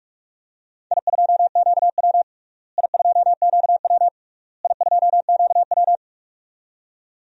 Morse code, heard as a keyed tone, about 41 wpm, 690 Hz. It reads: I2QW I2QW I2QW
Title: DXCC-100 Call Signs - 4 Characters - Repeated 3x - 45wpm